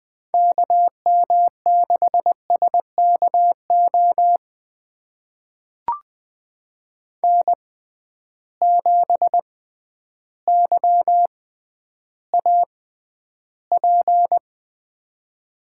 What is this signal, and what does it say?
Morse code, heard as a keyed tone, about 20 wpm, spaced as for 11 wpm, 700 Hz. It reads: KM6SKO N7YAP
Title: USA Call Signs - 6x Character Spacing - 20wpm